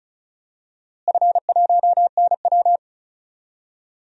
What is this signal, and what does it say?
Morse code, heard as a keyed tone, about 36 wpm, 700 Hz. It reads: F1NW
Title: DXCC-100 Call Signs - 4 Characters - Repeated 2x - 35wpm